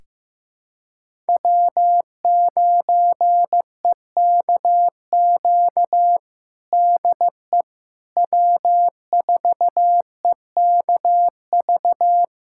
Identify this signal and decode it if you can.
Morse code, heard as a keyed tone, about 15 wpm, 700 Hz. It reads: W9EKQ DE W4EKV
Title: USA Callsign DE Callsign - 5 Characters - 15wpm